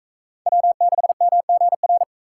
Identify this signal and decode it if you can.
Morse code, heard as a keyed tone, about 41 wpm, 700 Hz. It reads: W6MGR